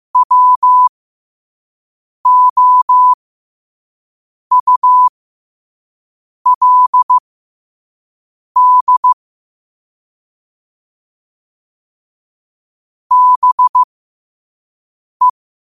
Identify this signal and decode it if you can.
Morse code, heard as a keyed tone, about 15 wpm, spaced as for 5 wpm, 1000 Hz. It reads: WOULD BE
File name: qotdfiles / 5wpm